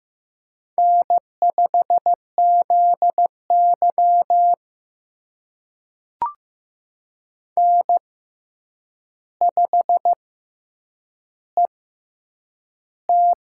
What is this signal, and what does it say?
Morse code, heard as a keyed tone, about 15 wpm, spaced as for 8 wpm, 700 Hz. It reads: N5ZY N5ET
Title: USA Call Signs - 4 Characters - 6x Character Spacing - 15wpm